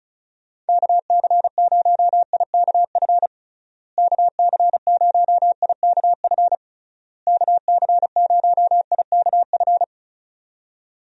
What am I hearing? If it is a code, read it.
Morse code, heard as a keyed tone, about 36 wpm, 700 Hz. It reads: KC0IKF KC0IKF KC0IKF
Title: USA Call Signs - 6 Characters - Repeated 3x - 35wpm